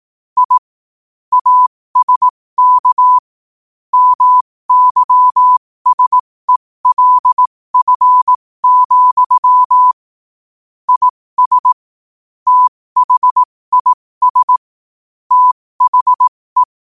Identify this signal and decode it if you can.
Morse code, heard as a keyed tone, about 18 wpm, spaced as for 15 wpm, 1000 Hz. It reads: I ASK MYSELF, IS THIS THE
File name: qotdfiles / 15wpm